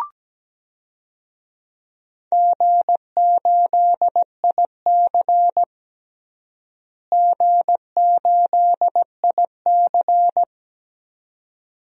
Morse code, heard as a keyed tone, about 17 wpm, 700 Hz. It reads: G8IC G8IC
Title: DXCC-100 Call Signs - 4 Characters - Repeated 2x - 17wpm